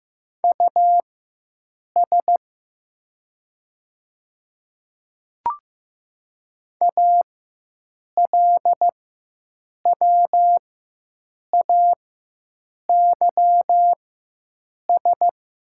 Morse code, heard as a keyed tone, about 15 wpm, spaced as for 7 wpm, 700 Hz. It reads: US ALWAYS
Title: 4x Character Spacing - Top 300 Words - 15wpm